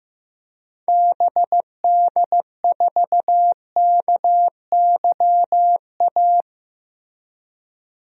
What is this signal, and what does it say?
Morse code, heard as a keyed tone, about 15 wpm, 700 Hz. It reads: BD4KYA
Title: DXCC-100 Call Signs - 6 Characters - Repeated 2x - 15wpm